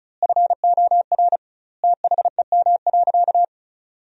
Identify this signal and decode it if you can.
Morse code, heard as a keyed tone, about 36 wpm, 700 Hz. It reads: FOR THEM.
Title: 2x Word-Spacing - Sentences from Top 300 Words - 35wpm